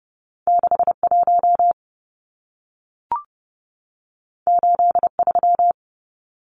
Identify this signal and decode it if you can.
Morse code, heard as a keyed tone, about 30 wpm, 700 Hz. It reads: 61 83
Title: Numbers - 2-Digit 30wpm